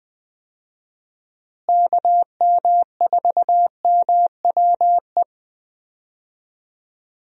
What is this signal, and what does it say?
Morse code, heard as a keyed tone, about 20 wpm, 700 Hz. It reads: KM4MWE
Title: USA Call Signs - 6 Characters - Repeated 3x - 20wpm